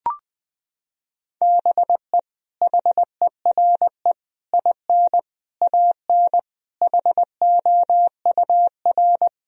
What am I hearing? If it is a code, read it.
Morse code, heard as a keyed tone, about 21 wpm, 700 Hz. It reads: BE HERE IN AN HOUR
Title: Sentences from Top 1000 Words - Review Part 4 - 20wpm